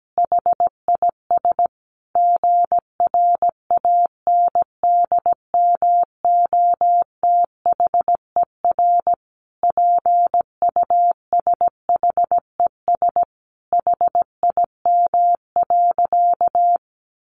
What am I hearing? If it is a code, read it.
Morse code, heard as a keyed tone, about 17 wpm, 700 Hz. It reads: HIS GRANDMOTHER PUSHES HIM.
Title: Additional Easy Phrases 17wpm